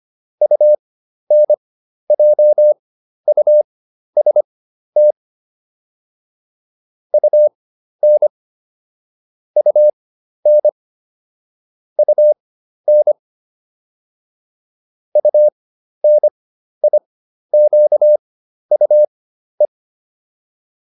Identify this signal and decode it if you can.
Morse code, heard as a keyed tone, about 25 wpm, spaced as for 12 wpm, 600 Hz. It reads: UNJUST UN UN UN UNIQUE
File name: UN6 12